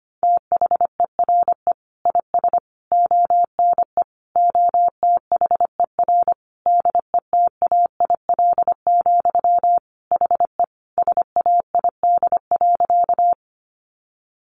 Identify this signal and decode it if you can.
Morse code, heard as a keyed tone, about 25 wpm, 700 Hz. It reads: THERE IS ONE OTHER DETAIL, HE SAID.
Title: Sentences from Top 1500 Words 25wpm